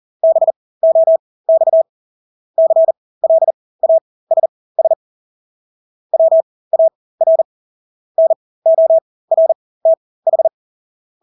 Morse code, heard as a keyed tone, about 40 wpm, spaced as for 20 wpm, 650 Hz. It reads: BOX CLASS WAR NORTH
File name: Engelske ord 40-20 wpm